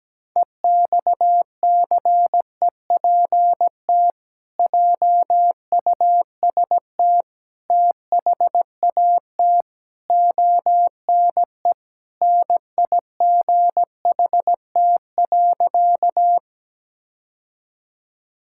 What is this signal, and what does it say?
Morse code, heard as a keyed tone, about 17 wpm, 700 Hz. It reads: EXCEPT JUST THAT ONE NIGHT.
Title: Sentences from Top 1600 Words 17wpm